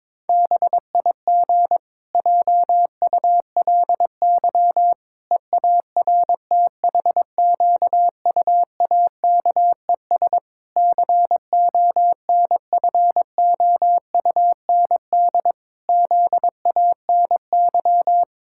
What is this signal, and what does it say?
Morse code, heard as a keyed tone, about 22 wpm, 700 Hz. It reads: BIG JULY EARTHQUAKES CONFOUND ZANY
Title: Pangram Challenge v2 - 22wpm